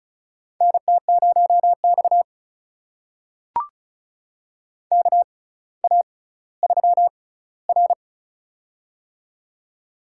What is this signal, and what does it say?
Morse code, heard as a keyed tone, about 34 wpm, spaced as for 18 wpm, 700 Hz. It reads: NT0X KA3R
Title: USA Call Signs - 4 Characters - 6x Character Spacing - 35wpm